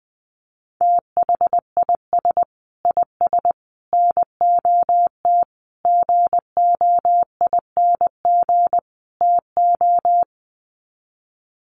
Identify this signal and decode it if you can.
Morse code, heard as a keyed tone, about 20 wpm, 700 Hz. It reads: THIS IS NOT GOING TO
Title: Sets of 5 Words Encore2 20wpm